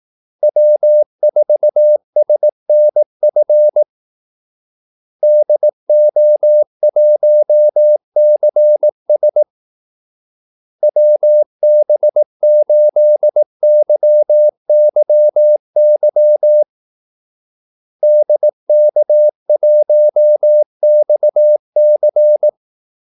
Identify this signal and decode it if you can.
Morse code, heard as a keyed tone, about 18 wpm, 600 Hz. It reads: W4SNF DO1CS WB8YYY DK1XC
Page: Kaldesignaler 18 wpm